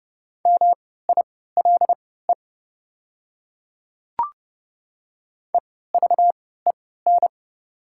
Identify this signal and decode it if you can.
Morse code, heard as a keyed tone, about 30 wpm, spaced as for 17 wpm, 700 Hz. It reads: MILE EVEN
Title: 3x Character Spacing - Top 300 Words - 30wpm